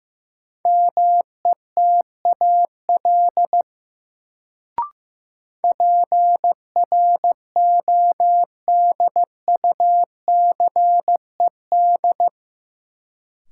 Morse code, heard as a keyed tone, about 15 wpm, 700 Hz. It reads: METAL PRODUCED
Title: Sentences from Top 1000 Words - Review Part 6 - New Word Drill - 15wpm